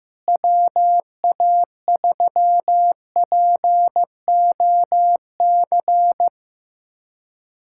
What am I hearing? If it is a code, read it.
Morse code, heard as a keyed tone, about 15 wpm, 700 Hz. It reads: WA3POC
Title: USA Callsign DE Callsign 15wpm